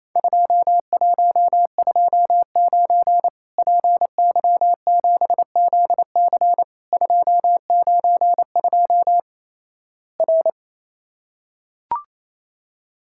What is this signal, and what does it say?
Morse code, heard as a keyed tone, about 28 wpm, 700 Hz. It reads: 2129 PY7ZC 292 R
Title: Worked All Europe DX Contest - QTC Exchange - 28wpm